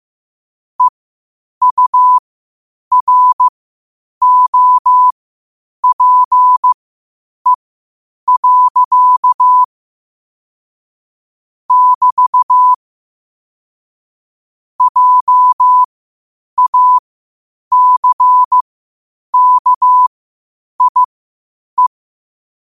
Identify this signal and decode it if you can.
Morse code, heard as a keyed tone, about 15 wpm, spaced as for 8 wpm, 1000 Hz. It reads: EUROPE. <BT> JACKIE
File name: qotdfiles / 8wpm